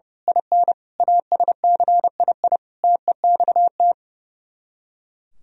Morse code, heard as a keyed tone, about 28 wpm, 700 Hz. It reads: IN ASCII TEXT
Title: Speed-Racing - Sets of 5 Words 30wpm